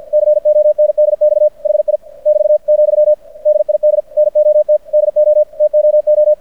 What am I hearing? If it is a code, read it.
Morse code, heard as a keyed tone, about 53 wpm, 600 Hz. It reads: CONDX HI <BT> , DID NOT DO TOO
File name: archived / CRSnet